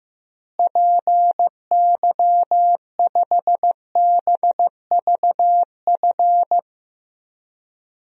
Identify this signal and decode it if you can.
Morse code, heard as a keyed tone, about 15 wpm, 700 Hz. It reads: PY5BVF
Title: DXCC-100 Call Signs - 6 Characters - Repeated 3x - 15wpm